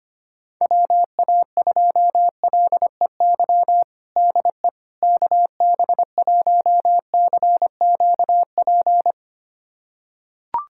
Morse code, heard as a keyed tone, about 25 wpm, 700 Hz. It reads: WA2LEY DE KB1CQP
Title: USA Callsign DE Callsign 25wpm